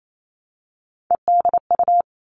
Morse code, heard as a keyed tone, about 28 wpm, 700 Hz. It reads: EDU
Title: All Cut Numbers - 3-Digit 28wpm